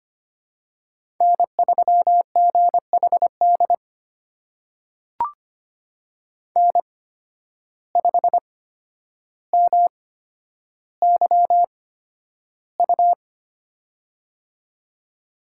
Morse code, heard as a keyed tone, about 25 wpm, spaced as for 11 wpm, 700 Hz. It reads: N3GHD N5MYU
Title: USA Call Signs - 5 Characters - 8x Character Spacing - 25wpm